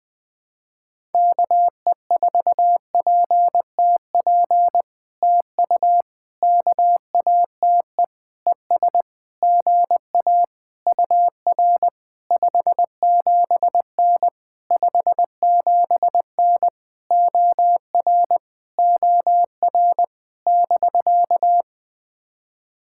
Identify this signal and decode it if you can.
Morse code, heard as a keyed tone, about 20 wpm, 700 Hz. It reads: KE4PTP TU KATE ES GA UR 57N 57N OR OR <BK>
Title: POTA QSO Practice - 20wpm